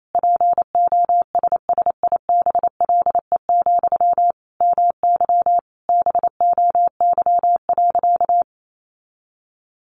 Morse code, heard as a keyed tone, about 28 wpm, 700 Hz. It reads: POSSIBLE, MY BOY.
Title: Sentences from Top 800 Words 28wpm